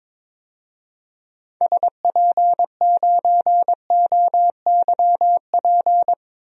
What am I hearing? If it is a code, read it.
Morse code, heard as a keyed tone, about 22 wpm, 700 Hz. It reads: SP9OYP
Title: DXCC-100 Call Signs - 6 Characters - 22wpm